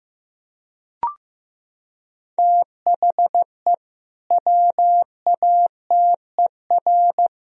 Morse code, heard as a keyed tone, about 15 wpm, 700 Hz. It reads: THE WATER
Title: Sentences from Top 1000 Words - Review Part 3 - 15wpm